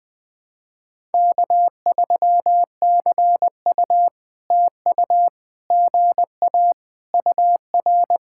Morse code, heard as a keyed tone, about 20 wpm, 700 Hz. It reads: K3CU TU GA UR
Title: POTA QSO Practice - 20wpm